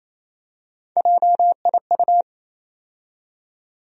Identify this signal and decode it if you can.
Morse code, heard as a keyed tone, about 28 wpm, 700 Hz. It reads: JIU
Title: USA Call Sign Postfixes - 3 Letters 28wpm